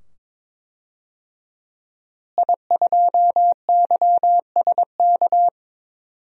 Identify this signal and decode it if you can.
Morse code, heard as a keyed tone, about 22 wpm, 700 Hz. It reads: I2YSK
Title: DXCC-100 Call Signs - 5 Characters - Repeated 2x - 22wpm